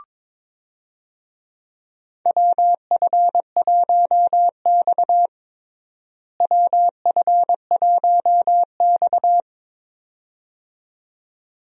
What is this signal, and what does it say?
Morse code, heard as a keyed tone, about 22 wpm, 700 Hz. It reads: WF1X WF1X
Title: USA Call Signs - 4 Characters - Repeated 2x - 22wpm